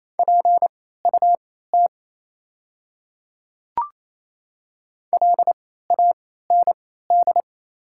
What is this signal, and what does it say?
Morse code, heard as a keyed tone, about 28 wpm, spaced as for 16 wpm, 700 Hz. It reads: PUT LAND
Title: 3x Character Spacing - Top 300 Words - 28wpm